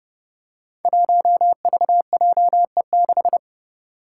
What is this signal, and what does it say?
Morse code, heard as a keyed tone, about 30 wpm, 700 Hz. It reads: 1VJE6
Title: Five Challenging Letter-Number Combinations 30wpm